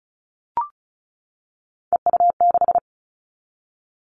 Morse code, heard as a keyed tone, about 36 wpm, 700 Hz. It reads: EU6
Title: Callsign Prefixes for Top 50 DXCC - 35wpm